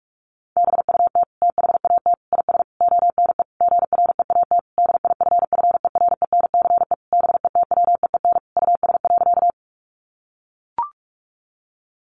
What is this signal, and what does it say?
Morse code, heard as a keyed tone, about 44 wpm, 700 Hz. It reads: BUT THAT IS ONE GREAT DIFFERENCE BETWEEN US.
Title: Sentences from Top 800 Words 45wpm